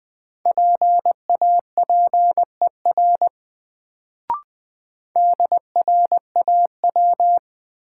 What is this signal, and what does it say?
Morse code, heard as a keyed tone, about 20 wpm, 700 Hz. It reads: PAPER DRAW
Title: Sentences from Top 300 Words - New Word Drill - 20wpm